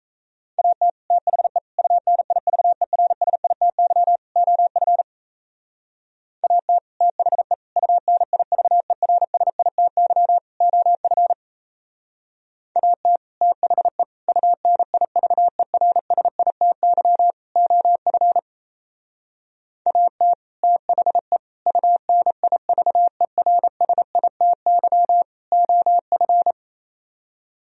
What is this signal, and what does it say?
Morse code, heard as a keyed tone, about 33 wpm, 700 Hz. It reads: AT THE UNIVERSITY OF AT THE UNIVERSITY OF AT THE UNIVERSITY OF AT THE UNIVERSITY OF
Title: Speed-Racing - Sets of 4 Words 28wpm